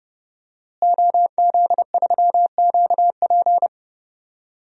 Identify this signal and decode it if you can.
Morse code, heard as a keyed tone, about 30 wpm, 700 Hz. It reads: OZ3QP
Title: DXCC-100 Call Signs - 5 Characters - Repeated 3x - 30wpm